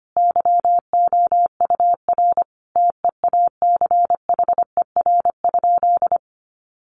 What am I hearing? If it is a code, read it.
Morse code, heard as a keyed tone, about 25 wpm, 700 Hz. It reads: YOUR TEACHER?
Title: Sentences from Top 1300 Words 25wpm